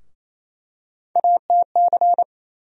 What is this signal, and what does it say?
Morse code, heard as a keyed tone, about 28 wpm, 700 Hz. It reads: ATC
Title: USA Call Sign Postfixes - 3 Letters 28wpm